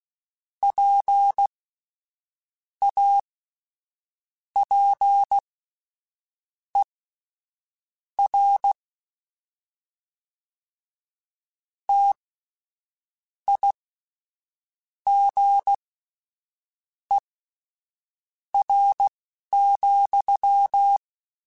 Morse code, heard as a keyed tone, about 16 wpm, spaced as for 6 wpm, 800 Hz. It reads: PAPER TIGER,
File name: Misc / Misc-mp3s / WordPairs sp @10